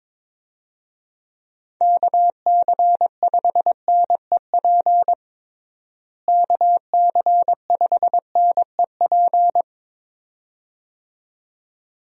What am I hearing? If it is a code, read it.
Morse code, heard as a keyed tone, about 21 wpm, 700 Hz. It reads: KC5NEP KC5NEP
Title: USA Call Signs - 6 Characters - Repeated 2x - 22wpm